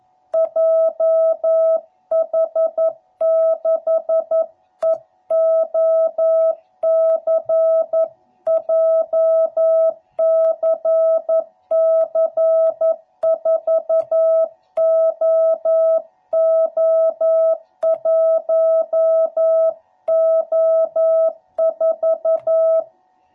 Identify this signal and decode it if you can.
Morse code, heard as a keyed tone, about 11 wpm, 660 Hz. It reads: JH6EOC JCC4OO1O4
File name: jh8cbh / 2021cw / cw12b